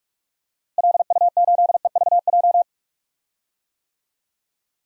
Morse code, heard as a keyed tone, about 45 wpm, 700 Hz. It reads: LU8EVJ
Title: DXCC-100 Call Signs - 6 Characters - Repeated 2x - 45wpm